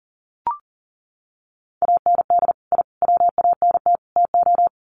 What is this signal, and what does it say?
Morse code, heard as a keyed tone, about 40 wpm, 700 Hz. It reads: AND I WANT TO
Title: Sets of 4 Words 40wpm